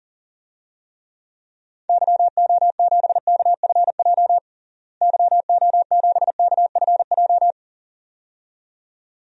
Morse code, heard as a keyed tone, about 40 wpm, 700 Hz. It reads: YO7KFJ YO7KFJ
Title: DXCC-100 Call Signs - 6 Characters - Repeated 2x - 40wpm